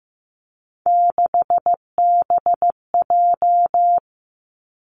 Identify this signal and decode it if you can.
Morse code, heard as a keyed tone, about 15 wpm, 700 Hz. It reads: 6BJ
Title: Three Challenging Letter-Number Combinations 15wpm